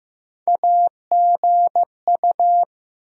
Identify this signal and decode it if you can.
Morse code, heard as a keyed tone, about 15 wpm, 700 Hz. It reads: AGU
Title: USA Call Sign Postfixes - 3 Letters 15wpm